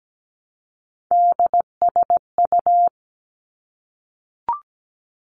Morse code, heard as a keyed tone, about 17 wpm, 700 Hz. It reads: DSU
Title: Three Challenging Letter-Number Combinations 17wpm